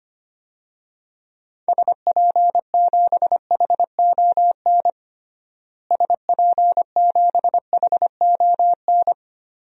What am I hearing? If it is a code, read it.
Morse code, heard as a keyed tone, about 25 wpm, 700 Hz. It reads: SP7HON SP7HON
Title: DXCC-100 Call Signs - 6 Characters - Repeated 2x - 25wpm